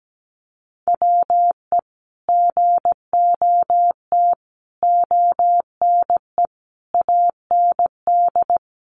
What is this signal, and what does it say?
Morse code, heard as a keyed tone, about 17 wpm, 700 Hz. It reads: WE GOT ONE AND
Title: Sentences from Top 300 Words 17wpm